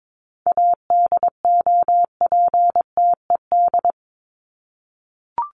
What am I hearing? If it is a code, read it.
Morse code, heard as a keyed tone, about 22 wpm, 700 Hz. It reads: ADOPTED
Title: ED Suffix - Top 100 Words with Base Words 22wpm